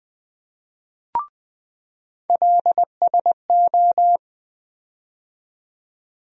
Morse code, heard as a keyed tone, about 20 wpm, 700 Hz. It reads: LSO
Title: USA Call Sign Postfixes - 3 Letters 20wpm